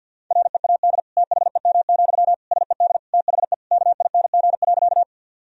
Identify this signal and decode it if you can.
Morse code, heard as a keyed tone, about 50 wpm, 700 Hz. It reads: READ THEM, SED THE KING.
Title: Sentences from Top 400 Words and Abbreviations 50wpm